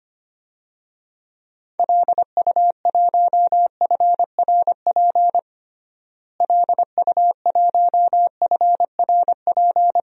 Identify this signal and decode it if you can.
Morse code, heard as a keyed tone, about 25 wpm, 700 Hz. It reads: LU1FRP LU1FRP
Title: DXCC-100 Call Signs - 6 Characters - Repeated 2x - 25wpm